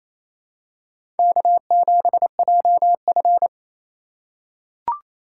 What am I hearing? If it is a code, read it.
Morse code, heard as a keyed tone, about 27 wpm, 700 Hz. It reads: K7JF
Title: USA Call Signs - 4 Characters - 4x Character Spacing - 28wpm